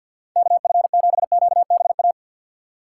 Morse code, heard as a keyed tone, about 48 wpm, 700 Hz. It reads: KF7QBA
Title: USA Call Signs - 2x Character Spacing - 50wpm